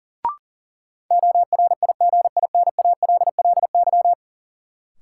Morse code, heard as a keyed tone, about 40 wpm, 700 Hz. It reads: ORIGINALLY
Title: Sentences from Top 1500 Words - New Word Drill Alt - 40wpm